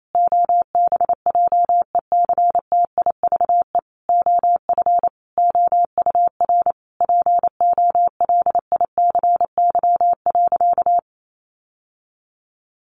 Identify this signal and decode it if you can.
Morse code, heard as a keyed tone, about 28 wpm, 700 Hz. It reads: OBJECTIVE OF OUR POLICY.
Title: Sentences from Top 1200 Words 28wpm